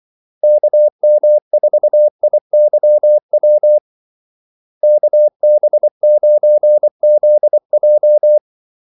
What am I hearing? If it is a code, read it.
Morse code, heard as a keyed tone, about 24 wpm, 600 Hz. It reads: KM4IYW KB9ZJ
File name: Callsigns-24wpm